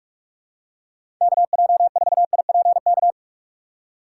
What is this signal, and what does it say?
Morse code, heard as a keyed tone, about 44 wpm, 700 Hz. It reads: KJ4IPK
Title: USA Call Signs - 8x Character Spacing - 45wpm